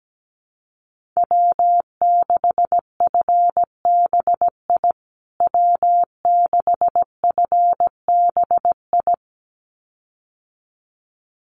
Morse code, heard as a keyed tone, about 17 wpm, 700 Hz. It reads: W6FBI W6FBI
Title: USA Call Signs Repeated 2x 17wpm